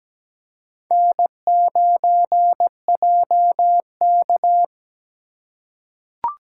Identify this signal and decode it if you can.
Morse code, heard as a keyed tone, about 17 wpm, 700 Hz. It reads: N9JK
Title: USA Call Signs - 4 Characters - 3x Character Spacing - 17wpm